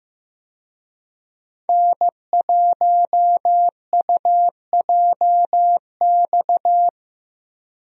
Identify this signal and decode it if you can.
Morse code, heard as a keyed tone, about 15 wpm, 700 Hz. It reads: N1UJX
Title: USA Call Signs - 5 Characters - Repeated 2x - 15wpm